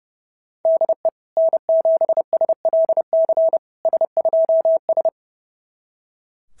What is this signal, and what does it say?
Morse code, heard as a keyed tone, about 26 wpm, 650 Hz. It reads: DE N7SLC S2S
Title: SOTA QSO Practice 30wpm